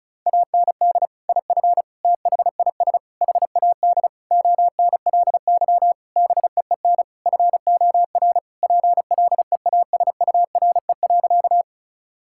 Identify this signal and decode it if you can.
Morse code, heard as a keyed tone, about 36 wpm, 700 Hz. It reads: AND IF THIS HAD ONLY BEEN FOR PLEASURE.
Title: Sentences from Top 1600 Words 35wpm